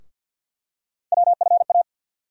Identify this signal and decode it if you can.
Morse code, heard as a keyed tone, about 47 wpm, 700 Hz. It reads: WFA
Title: USA Call Sign Postfixes - 3 Letters 50wpm